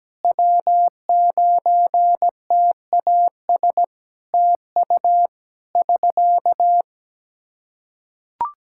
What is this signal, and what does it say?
Morse code, heard as a keyed tone, about 17 wpm, 700 Hz. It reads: W9TAS TU <SK>
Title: Practice QSO - Exchange 4 - Ending - 17wpm